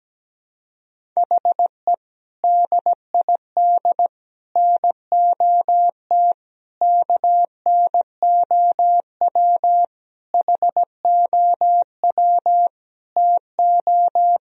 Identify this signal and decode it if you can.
Morse code, heard as a keyed tone, about 17 wpm, 700 Hz. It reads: HE DID NOT KNOW HOW TO
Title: Sentences from Top 1000 Words - Review Part 1 - 17wpm